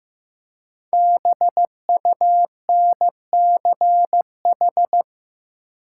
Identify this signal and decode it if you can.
Morse code, heard as a keyed tone, about 15 wpm, 700 Hz. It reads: BUNCH
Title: Five Letter Words - 15wpm